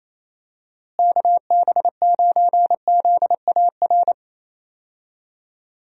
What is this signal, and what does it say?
Morse code, heard as a keyed tone, about 28 wpm, 700 Hz. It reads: KB9ZAR
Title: USA Call Signs - 6 Characters - Repeated 3x - 28wpm